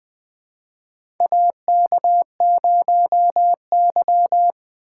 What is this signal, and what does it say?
Morse code, heard as a keyed tone, about 20 wpm, 700 Hz. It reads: AK0Y
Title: USA Call Signs - 4 Characters - 2x Character Spacing - 20wpm